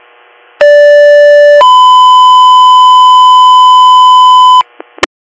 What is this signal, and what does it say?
Two-tone page, heard as a one-second tone, then a three-second tone, about 580 then 1010 Hz, with noise. It is heard from 0.6 to 4.6 s.